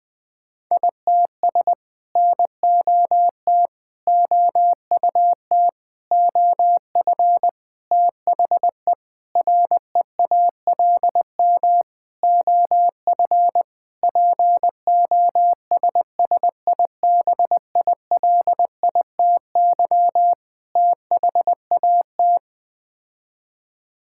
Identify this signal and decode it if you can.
Morse code, heard as a keyed tone, about 20 wpm, 700 Hz. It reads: ITS NOT OUT OF THE REALM OF POSSIBILITY THAT
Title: Sets of 9 Words - 20wpm